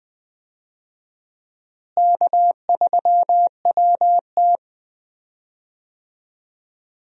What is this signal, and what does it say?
Morse code, heard as a keyed tone, about 20 wpm, 700 Hz. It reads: K3WT
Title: USA Call Signs - 4 Characters - Repeated 3x - 20wpm